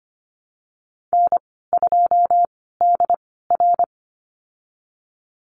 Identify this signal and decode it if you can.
Morse code, heard as a keyed tone, about 25 wpm, spaced as for 16 wpm, 700 Hz. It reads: N2DR
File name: Lesson 15g - Number 5 - Call Signs 25wpm